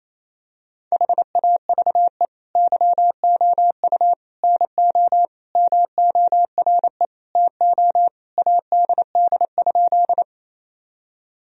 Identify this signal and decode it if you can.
Morse code, heard as a keyed tone, about 28 wpm, 700 Hz. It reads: HAVE YOU NO MORE TO ADD?
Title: Sentences from Top 1000 Words - Review Part 1 - 28wpm